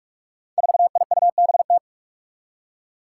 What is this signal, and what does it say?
Morse code, heard as a keyed tone, about 46 wpm, 700 Hz. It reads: 4IUBT
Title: Five Challenging Letter-Number Combinations 45wpm